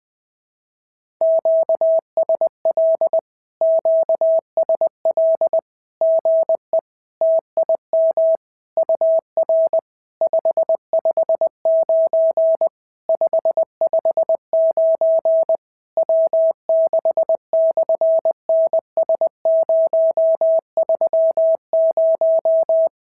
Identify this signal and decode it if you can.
Morse code, heard as a keyed tone, about 20 wpm, 650 Hz. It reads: QSL QSL GE TIM UR 559 559 W6/NS030
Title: SOTA QSO Practice 20wpm